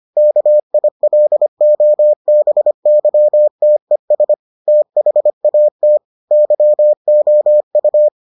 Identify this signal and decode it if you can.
Morse code, heard as a keyed tone, about 25 wpm, 600 Hz. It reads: KILOBYTES THAT YOU